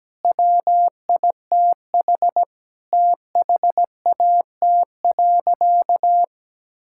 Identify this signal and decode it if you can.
Morse code, heard as a keyed tone, about 17 wpm, 700 Hz. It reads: WITH THAT.
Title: Sentences from Top 1000 Words - Review Part 5 - 17wpm